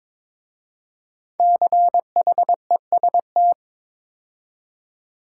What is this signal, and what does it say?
Morse code, heard as a keyed tone, about 23 wpm, 700 Hz. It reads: CHEST
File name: 5 Letter Words - First 3 Letters Given - 22wpm